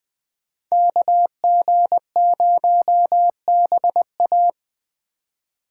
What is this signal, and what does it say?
Morse code, heard as a keyed tone, about 21 wpm, 700 Hz. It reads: KG0BA
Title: USA Call Signs - 5 Characters - 4x Character Spacing - 20wpm